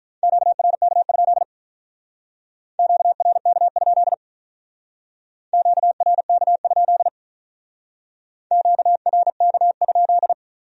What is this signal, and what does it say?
Morse code, heard as a keyed tone, about 45 wpm, 700 Hz. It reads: QRK? QRK? QRK? QRK?